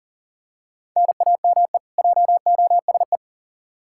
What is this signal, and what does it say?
Morse code, heard as a keyed tone, about 39 wpm, 700 Hz. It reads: NAME JOSE
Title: Practice QSO - Exchange 1 - Beginning - 40wpm